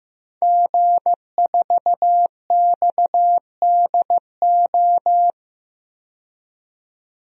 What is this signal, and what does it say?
Morse code, heard as a keyed tone, about 15 wpm, 700 Hz. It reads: G4XDO
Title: DXCC-100 Call Signs - 5 Characters - 15wpm